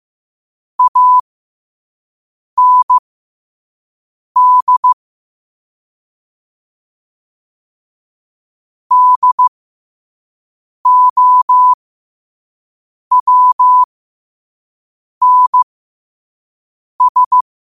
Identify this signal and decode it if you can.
Morse code, heard as a keyed tone, about 15 wpm, spaced as for 5 wpm, 1000 Hz. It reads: AND DOWNS
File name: 5wpm